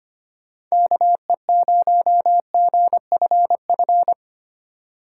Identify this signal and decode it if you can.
Morse code, heard as a keyed tone, about 25 wpm, 700 Hz. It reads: KE0GFF
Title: USA Call Signs - 2x Character Spacing - 25wpm